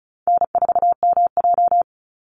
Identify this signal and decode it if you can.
Morse code, heard as a keyed tone, about 36 wpm, 700 Hz. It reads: N4MJ
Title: NA SOTA Chaser Callsigns 35wpm